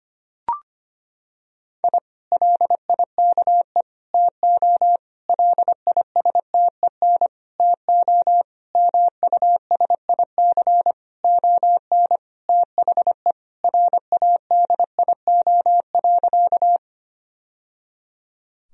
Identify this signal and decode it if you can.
Morse code, heard as a keyed tone, about 25 wpm, 700 Hz. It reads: I LIKE TO LISTEN TO MUSIC ON THE RADIO.
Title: Sentences from Top 1000 Words - Review Part 3 - 25wpm